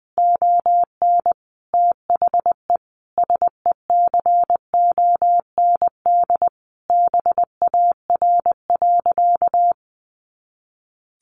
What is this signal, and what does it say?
Morse code, heard as a keyed tone, about 20 wpm, 700 Hz. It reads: ON THE SECOND BAR.
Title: Easy Phrases 20wpm